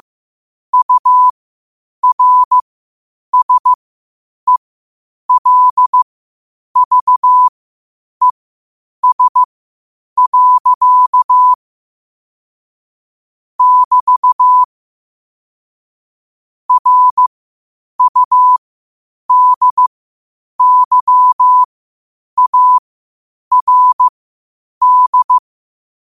Morse code, heard as a keyed tone, about 15 wpm, spaced as for 8 wpm, 1000 Hz. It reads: URSELVES. <BT> RUDYARD
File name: qotdfiles / 8wpm